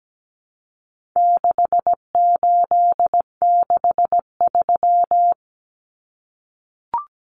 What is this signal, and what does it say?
Morse code, heard as a keyed tone, about 17 wpm, 700 Hz. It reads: 6863
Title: Common Cut Numbers - 4-Digit 17wpm